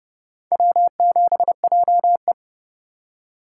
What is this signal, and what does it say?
Morse code, heard as a keyed tone, about 28 wpm, 700 Hz. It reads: W7JE